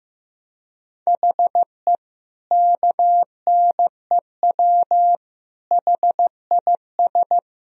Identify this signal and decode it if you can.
Morse code, heard as a keyed tone, about 15 wpm, 700 Hz. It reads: HE KNEW HIS
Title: Sentences from Top 1000 Words - Review Part 1 - 15wpm